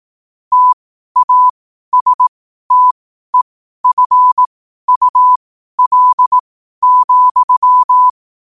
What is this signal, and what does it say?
Morse code, heard as a keyed tone, about 18 wpm, spaced as for 12 wpm, 1000 Hz. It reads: TASTEFUL,
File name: qotdfiles / 12wpm